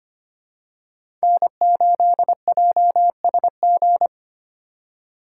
Morse code, heard as a keyed tone, about 25 wpm, 700 Hz. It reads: N8JSG
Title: USA Call Signs - 5 Characters - 2x Character Spacing - 25wpm